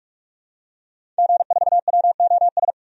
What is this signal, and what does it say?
Morse code, heard as a keyed tone, about 46 wpm, 700 Hz. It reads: G4WOS
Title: DXCC-100 Call Signs - 5 Characters - 45wpm